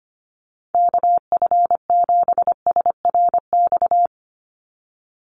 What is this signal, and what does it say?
Morse code, heard as a keyed tone, about 25 wpm, 700 Hz. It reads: KF7SRX
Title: USA Call Signs 25wpm